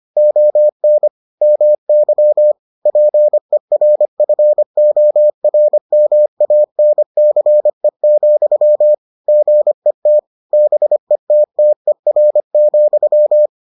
Morse code, heard as a keyed tone, about 25 wpm, 600 Hz. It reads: ON MY PERFORMANCE, GET BETTER,